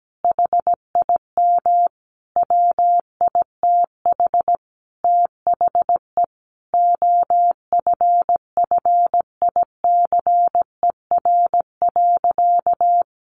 Morse code, heard as a keyed tone, about 17 wpm, 700 Hz. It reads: HIM WITH THE OFFICER.
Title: Sentences from Top 1200 Words 17wpm